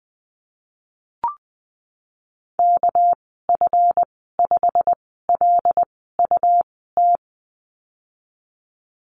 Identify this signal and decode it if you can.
Morse code, heard as a keyed tone, about 20 wpm, spaced as for 15 wpm, 700 Hz. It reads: KF5LUT
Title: 207 Lesson 33g - Letter j - Call Signs 20wpm